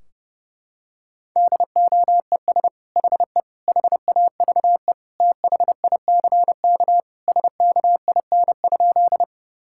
Morse code, heard as a keyed tone, about 32 wpm, 700 Hz. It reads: DOES HE HAVE THICK SKIN?
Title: Sentences from Top 1600 Words 30wpm